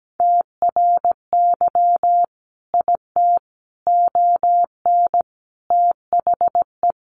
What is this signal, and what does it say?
Morse code, heard as a keyed tone, about 17 wpm, 700 Hz. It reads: TRY IT ON THE
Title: Sentences from Top 400 Words 17wpm